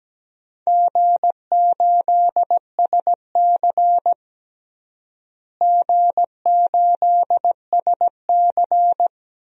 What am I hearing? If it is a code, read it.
Morse code, heard as a keyed tone, about 17 wpm, 700 Hz. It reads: G8SC G8SC
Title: DXCC-100 Call Signs - 4 Characters - Repeated 3x - 17wpm